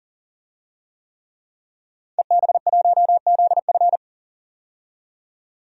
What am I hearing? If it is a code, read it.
Morse code, heard as a keyed tone, about 40 wpm, 700 Hz. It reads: EB1ZF
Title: DXCC-100 Call Signs - 5 Characters - 40wpm